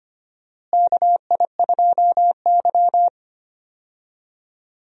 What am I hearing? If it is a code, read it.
Morse code, heard as a keyed tone, about 25 wpm, 700 Hz. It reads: KI2Y
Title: USA Call Signs - 4 Characters - Repeated 2x - 25wpm